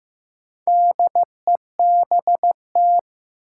Morse code, heard as a keyed tone, about 15 wpm, 700 Hz. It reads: DEBT
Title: Sentences from Top 1300 Words - New Word Drill Alt - 15wpm